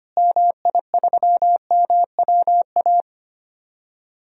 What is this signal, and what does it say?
Morse code, heard as a keyed tone, about 25 wpm, 700 Hz. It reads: MI3MWA
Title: DXCC-100 Call Signs - 6 Characters - Repeated 3x - 25wpm